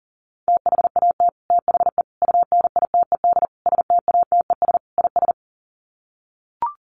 Morse code, heard as a keyed tone, about 40 wpm, 700 Hz. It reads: THAT THE UNITED STATES IS
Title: Sets of 5 Words Encore 40wpm